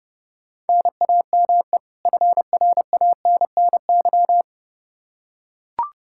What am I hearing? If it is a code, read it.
Morse code, heard as a keyed tone, about 30 wpm, 700 Hz. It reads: NAME FRANNY
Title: Practice QSO - Exchange 1 - Beginning - 30wpm